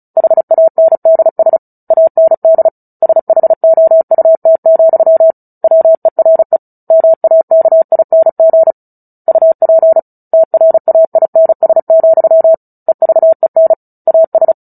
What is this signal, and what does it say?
Morse code, heard as a keyed tone, about 34 wpm, 650 Hz. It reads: HANDS AND SHOUT, WERE MAKING UP TRAINS, EVEN AS
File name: Book Two / 09-Chapter 9 - 35 WPM